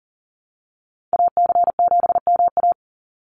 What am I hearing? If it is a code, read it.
Morse code, heard as a keyed tone, about 40 wpm, 700 Hz. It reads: AC7MA
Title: NA SOTA Activator Callsigns 40wpm